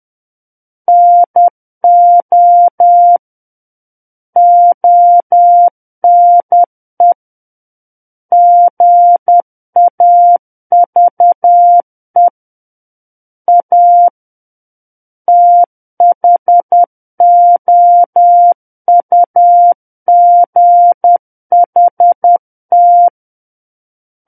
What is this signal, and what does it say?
Morse code, heard as a keyed tone, about 10 wpm, 700 Hz. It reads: NO ONE GAVE A THOUGHT
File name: Book One / 01-Chapter 1 - 10 WPM